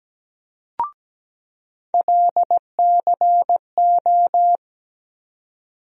Morse code, heard as a keyed tone, about 17 wpm, 700 Hz. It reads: LCO
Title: USA Call Sign Postfixes - 3 Letters 17wpm